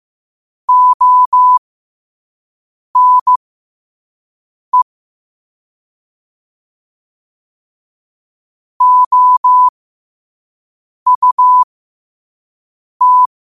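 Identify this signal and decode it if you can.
Morse code, heard as a keyed tone, about 15 wpm, spaced as for 5 wpm, 1000 Hz. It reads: ONE OUT
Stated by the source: Morse code at 5 wpm.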